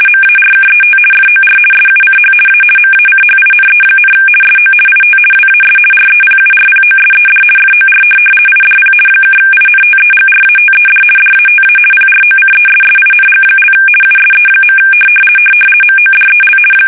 RTTY, about 50 baud. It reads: MARKET FOR SUCH A MACHINE, SO IT WAS REPURPOSED FOR IDP (INTEGRATED DATA PROCESSING) USE. MODEL
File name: fsk850-50B / teletype-corp